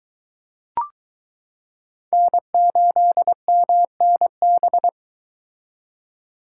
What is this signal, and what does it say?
Morse code, heard as a keyed tone, about 22 wpm, 700 Hz. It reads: N8MNB